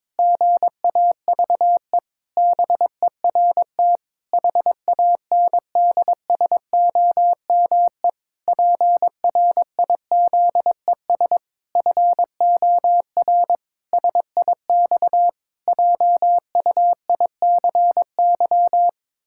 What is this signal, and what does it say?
Morse code, heard as a keyed tone, about 22 wpm, 700 Hz. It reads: GAVE BERT HANDSOME PRIZES FOR SIX JUICY
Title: Pangram Challenge v2 - 22wpm